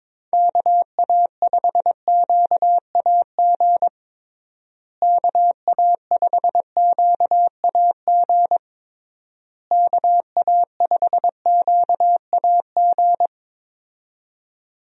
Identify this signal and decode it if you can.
Morse code, heard as a keyed tone, about 22 wpm, 700 Hz. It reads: KA5QAG KA5QAG KA5QAG
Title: USA Call Signs - 6 Characters - Repeated 3x - 22wpm